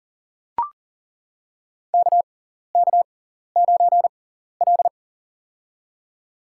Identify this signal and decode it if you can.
Morse code, heard as a keyed tone, about 40 wpm, spaced as for 14 wpm, 700 Hz. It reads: KK9L
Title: USA Call Signs - 6x Character Spacing - 40wpm